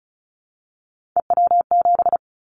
Morse code, heard as a keyed tone, about 36 wpm, 700 Hz. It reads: EW7
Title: Callsign Prefixes for Top 100 DXCC - 35wpm